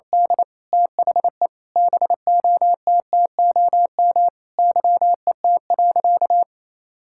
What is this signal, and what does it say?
Morse code, heard as a keyed tone, about 28 wpm, 700 Hz. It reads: D THE BOTTOM YET.
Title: Sentences from Top 1000 Words - Review Part 6 - 28wpm